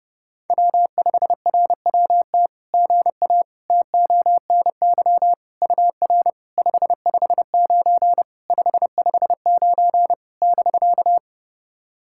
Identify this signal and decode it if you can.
Morse code, heard as a keyed tone, about 30 wpm, 700 Hz. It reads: W5RWT GA TONY UR 559 559 <BK>
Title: SOTA QSO Practice 30wpm